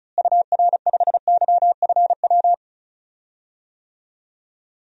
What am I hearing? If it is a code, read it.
Morse code, heard as a keyed tone, about 34 wpm, 700 Hz. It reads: UR5YFW
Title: DXCC-100 Call Signs - 6 Characters - Repeated 2x - 35wpm